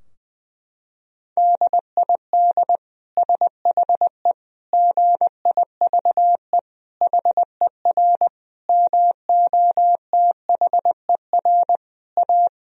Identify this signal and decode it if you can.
Morse code, heard as a keyed tone, about 21 wpm, 700 Hz. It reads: DID SHE GIVE HER MOTHER A
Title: Sentences from Top 1000 Words - Review Part 4 - 20wpm